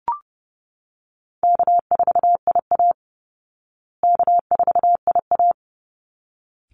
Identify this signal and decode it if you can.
Morse code, heard as a keyed tone, about 28 wpm, 700 Hz. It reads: K4IA K4IA
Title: Easy Call Signs Repeated 2x - 30wpm